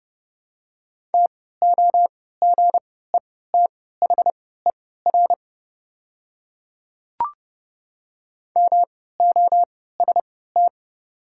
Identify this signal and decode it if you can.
Morse code, heard as a keyed tone, about 31 wpm, spaced as for 17 wpm, 700 Hz. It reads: TOGETHER MOST
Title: 3x Character Spacing - Top 300 Words - 30wpm